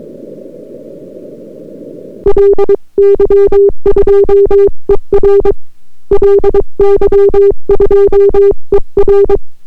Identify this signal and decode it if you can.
Morse code, heard as a keyed tone, about 22 wpm, 400 Hz. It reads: LY2ER LY2ER